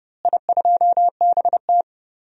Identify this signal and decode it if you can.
Morse code, heard as a keyed tone, about 32 wpm, 700 Hz. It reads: I2BT
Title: DXCC-100 Call Signs - 4 Characters - Repeated 3x - 30wpm